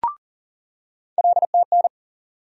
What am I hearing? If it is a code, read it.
Morse code, heard as a keyed tone, about 40 wpm, 700 Hz. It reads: LTN
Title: USA Call Sign Postfixes - 3 Letters 40wpm